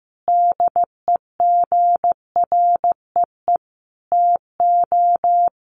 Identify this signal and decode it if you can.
Morse code, heard as a keyed tone, about 15 wpm, 700 Hz. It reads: DEGREE TO
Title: Sets of 5 Words Encore2 15wpm